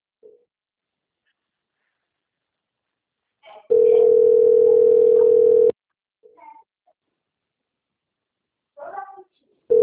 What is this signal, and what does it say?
Ringback tone (North America), 440 + 480 Hz.
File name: out- - - - - .19843